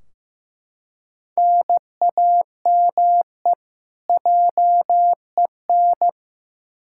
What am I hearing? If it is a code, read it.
Morse code, heard as a keyed tone, about 16 wpm, 700 Hz. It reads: NAME JEN
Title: K1USN SST QSO Practice 15wpm